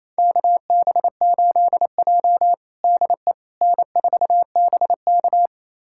Morse code, heard as a keyed tone, about 28 wpm, 700 Hz. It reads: KB8J DE N4BK
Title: USA Callsign DE Callsign - 4 Characters - 28wpm